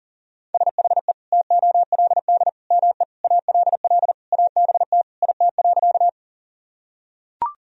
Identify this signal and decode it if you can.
Morse code, heard as a keyed tone, about 40 wpm, 700 Hz. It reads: SHE TOLD ME ALL ABT IT.
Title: Sentences from Top 400 Words and Abbreviations 40wpm